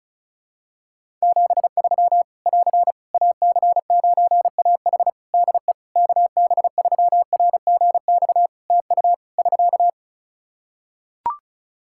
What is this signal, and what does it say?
Morse code, heard as a keyed tone, about 34 wpm, 700 Hz. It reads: 73 <AR> AC9AH DE KB3RGX TU <SK>
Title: Practice QSO - Exchange 4 - Ending - 35wpm